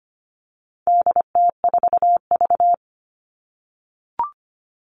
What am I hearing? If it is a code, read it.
Morse code, heard as a keyed tone, about 25 wpm, 700 Hz. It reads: DT4V
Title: All Cut Numbers - 4-Digit 25wpm